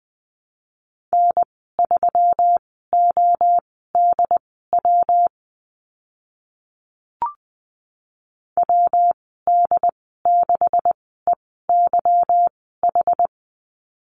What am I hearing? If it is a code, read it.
Morse code, heard as a keyed tone, about 20 wpm, spaced as for 14 wpm, 700 Hz. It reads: N3ODW WD6EYH
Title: 179 Lesson 29g - Slash - Call Signs 20wpm